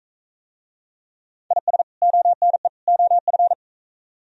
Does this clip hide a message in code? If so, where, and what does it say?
Yes: Morse code from 1.5 to 3.5 s, heard as a keyed tone, about 39 wpm, 700 Hz. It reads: IS ONE OF